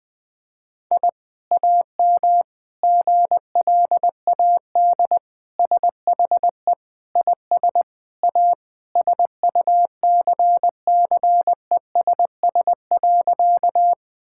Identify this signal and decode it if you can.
Morse code, heard as a keyed tone, about 21 wpm, 700 Hz. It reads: I AM GLAD SHE IS A SUCCESS.
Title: Sentences from Top 1000 Words - Review Part 6 - 20wpm